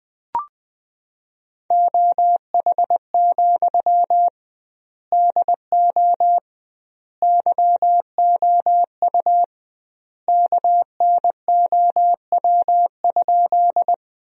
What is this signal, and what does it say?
Morse code, heard as a keyed tone, about 20 wpm, 700 Hz. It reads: OH, DO YOU KNOW?
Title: 2x Word-Spacing - Sentences from Top 500 Words - 20wpm